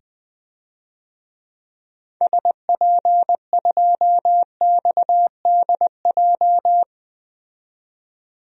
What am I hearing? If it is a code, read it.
Morse code, heard as a keyed tone, about 20 wpm, 700 Hz. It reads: SP2XDJ
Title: DXCC-100 Call Signs - 6 Characters - 20wpm